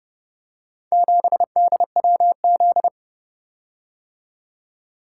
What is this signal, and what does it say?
Morse code, heard as a keyed tone, about 31 wpm, 700 Hz. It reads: 7DWZ